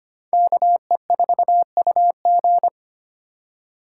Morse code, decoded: KE4UG

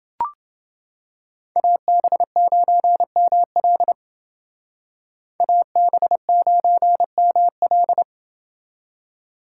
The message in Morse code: AB9ML AB9ML

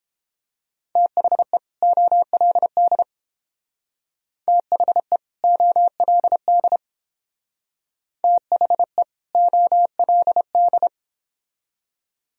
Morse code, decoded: THE OLD THE OLD THE OLD

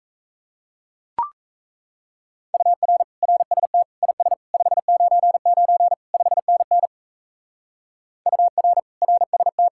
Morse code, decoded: UR RST IS 599 5NN UR RST

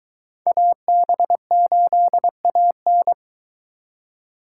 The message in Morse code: AB8AN

